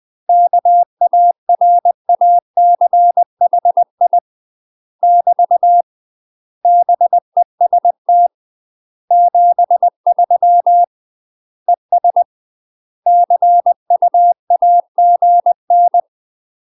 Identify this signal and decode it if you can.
Morse code, heard as a keyed tone, about 20 wpm, 700 Hz. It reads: KARACHI <BT> BEST 73 ES CUAGN
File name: UK MASTER QSOs TRAINING/20WPM